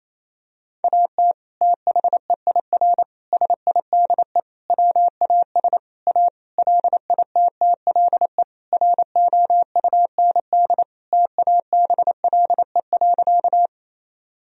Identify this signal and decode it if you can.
Morse code, heard as a keyed tone, about 28 wpm, 700 Hz. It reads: AT THEIR SIDE WAS A LITTLE ROUND TABLE.
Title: Sentences from Top 1000 Words - Review Part 2 - 28wpm